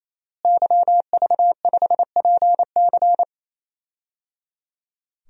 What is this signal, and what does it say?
Morse code, heard as a keyed tone, about 28 wpm, 700 Hz. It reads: YV5PC
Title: DXCC-100 Call Signs - 5 Characters - Repeated 3x - 28wpm